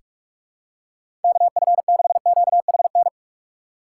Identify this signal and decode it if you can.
Morse code, heard as a keyed tone, about 46 wpm, 700 Hz. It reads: KF6QHN
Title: USA Call Signs - 6 Characters - 4x Character Spacing - 45wpm